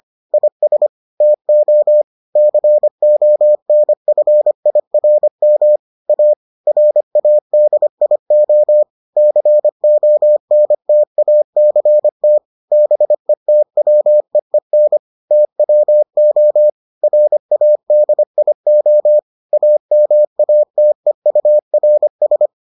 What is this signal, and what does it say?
Morse code, heard as a keyed tone, about 25 wpm, 600 Hz. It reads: IS TO CONFIRM A RADIO CONTACT BETWEEN TWO RADIO AMATEURS